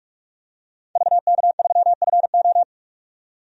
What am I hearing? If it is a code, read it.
Morse code, heard as a keyed tone, about 46 wpm, 700 Hz. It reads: VK3FO
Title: DXCC-100 Call Signs - 5 Characters - 45wpm